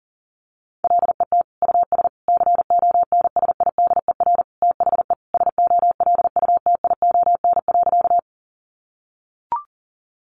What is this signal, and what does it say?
Morse code, heard as a keyed tone, about 40 wpm, 700 Hz. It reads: LET US CONSIDER THE SOLUTION.
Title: Sentences from Top 1300 Words 40wpm